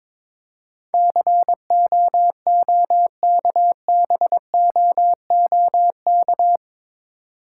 Morse code, heard as a keyed tone, about 23 wpm, 700 Hz. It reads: COOKBOOK
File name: Difficult Words with Hints - 22wpm